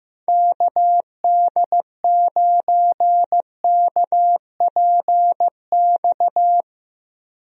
Morse code, heard as a keyed tone, about 15 wpm, 700 Hz. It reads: KD9KPX
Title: USA Call Signs - 6 Characters - Repeated 3x - 15wpm